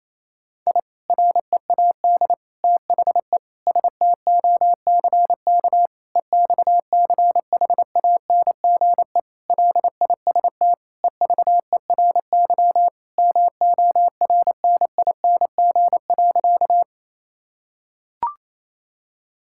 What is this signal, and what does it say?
Morse code, heard as a keyed tone, about 28 wpm, 700 Hz. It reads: I READ THE STOCK EXCHANGE LIST EVERY MORNING.
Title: Sentences from Top 1000 Words - Review Part 6 - 28wpm